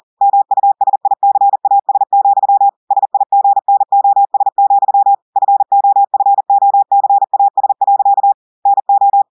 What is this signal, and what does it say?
Morse code, heard as a keyed tone, about 40 wpm, 800 Hz. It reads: MUSICAS, SIGNOS, FOFOCAS. NO
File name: audios / portugais / portugais texte en clair 2 vitesse 40 mots minutes